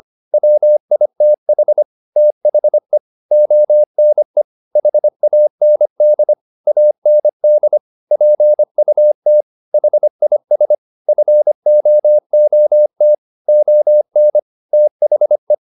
Morse code, heard as a keyed tone, about 25 wpm, 600 Hz. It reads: WITH THE ONE HAND AND PUT HIS FOOT ON THE
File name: the old man and the sea part 1 25wpm